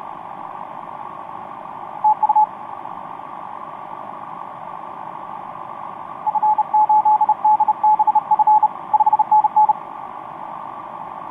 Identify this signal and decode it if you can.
Morse code, heard as a keyed tone, about 31 wpm, 860 Hz, with radio noise. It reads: TU F8DBF 5NN
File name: t46rrc i2uua etc 12m cw 18feb2013 1528